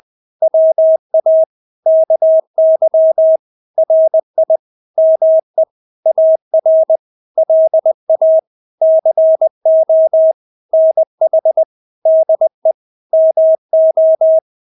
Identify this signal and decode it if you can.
Morse code, heard as a keyed tone, about 20 wpm, 640 Hz. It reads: WA KY RI ME AR LA CO NH DE MO
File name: State-Abbreviations-20